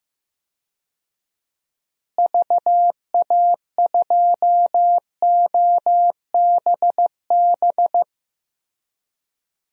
Morse code, heard as a keyed tone, about 15 wpm, 700 Hz. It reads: VA2OBB
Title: DXCC-100 Call Signs - 6 Characters - Repeated 3x - 15wpm